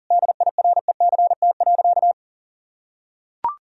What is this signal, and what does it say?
Morse code, heard as a keyed tone, about 40 wpm, 700 Hz. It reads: DIRECT.
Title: Sentences from Top 1000 Words - Review Part 2 - 40wpm